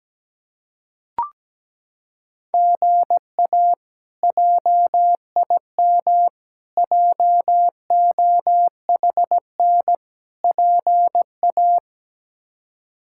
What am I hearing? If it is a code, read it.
Morse code, heard as a keyed tone, about 17 wpm, 700 Hz. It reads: GA JIM JOHN PA
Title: K1USN SST QSO Practice - 17wpm